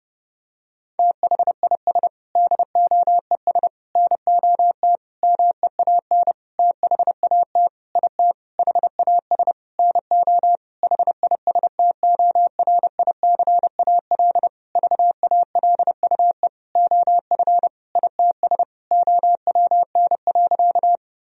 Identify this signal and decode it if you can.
Morse code, heard as a keyed tone, about 30 wpm, 700 Hz. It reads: THIS DOES NOT MEAN THAT IT HAS NO HISTORICAL VALUE OF ITS OWN.
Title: Sentences from Top 1000 Words - Review Part 4 - 30wpm